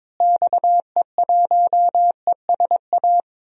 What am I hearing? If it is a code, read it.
Morse code, heard as a keyed tone, about 22 wpm, 700 Hz. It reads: XE1ESA
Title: DXCC-100 Call Signs - 6 Characters - Repeated 2x - 22wpm